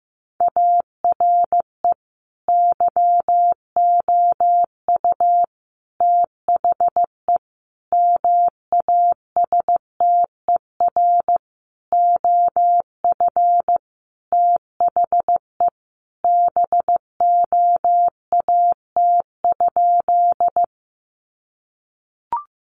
Morse code, heard as a keyed tone, about 15 wpm, 700 Hz. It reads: ARE YOU THE MASTER OF THE BOAT?
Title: Sentences from Top 1300 Words 15wpm